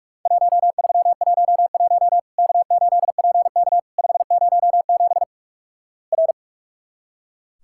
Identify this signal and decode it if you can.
Morse code, heard as a keyed tone, about 46 wpm, 700 Hz. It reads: 1311 K8PK 507 R
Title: Worked All Europe DX Contest - QTC Exchange - 45wpm